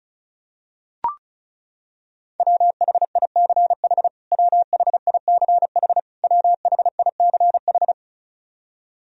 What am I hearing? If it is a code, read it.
Morse code, heard as a keyed tone, about 36 wpm, 700 Hz. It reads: WHICH WHICH WHICH
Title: Top 100 Words Repeated 35wpm